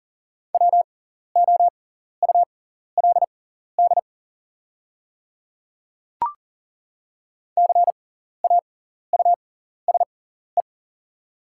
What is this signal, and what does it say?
Morse code, heard as a keyed tone, about 40 wpm, spaced as for 14 wpm, 700 Hz. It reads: WOULD CAUSE
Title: 6x Character Spacing - Top 200 Words - 40wpm